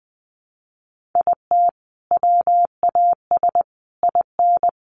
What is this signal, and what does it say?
Morse code, heard as a keyed tone, about 20 wpm, 700 Hz. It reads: IT WAS IN
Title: Sentences from Top 1400 Words 20wpm